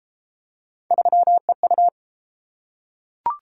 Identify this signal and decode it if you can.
Morse code, heard as a keyed tone, about 27 wpm, 700 Hz. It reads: 3EU
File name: Speed-Racing - Three Letter-Number Combinations - 22wpm